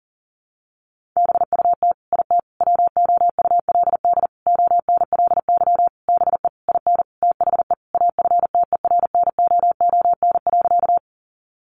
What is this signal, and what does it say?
Morse code, heard as a keyed tone, about 40 wpm, 700 Hz. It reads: BUT IT WOULD ONLY BE IN THE AFTERNOON.
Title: Sentences from Top 1500 Words 40wpm